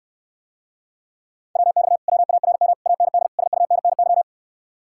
Morse code, heard as a keyed tone, about 69 wpm, 700 Hz. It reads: WX WARM AND SUNNY